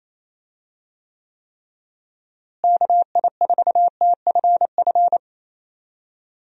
Morse code, heard as a keyed tone, about 28 wpm, 700 Hz. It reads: KI4TFF
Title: USA Call Signs - 6 Characters - Repeated 3x - 28wpm